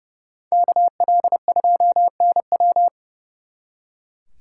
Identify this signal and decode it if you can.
Morse code, heard as a keyed tone, about 34 wpm, 700 Hz. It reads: KL2NW
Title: Speed Racing - USA Call Signs 30wpm